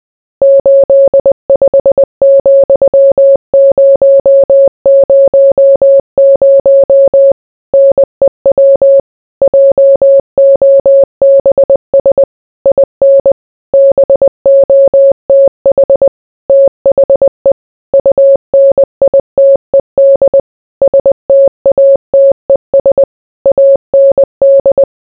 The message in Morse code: 85,000 NEW JOBS IN BOTH THE UNITED STATES AND

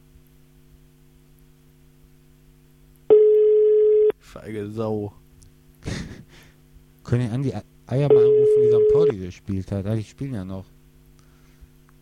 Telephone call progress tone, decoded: ringback tone

